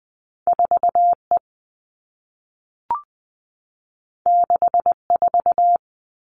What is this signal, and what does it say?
Morse code, heard as a keyed tone, about 20 wpm, 700 Hz. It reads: 4E 64
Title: All Cut Numbers - 2-Digit 20wpm